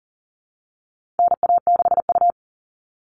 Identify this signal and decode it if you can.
Morse code, heard as a keyed tone, about 40 wpm, 700 Hz. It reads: NA6U
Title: All Cut Numbers - 4-Digit 40wpm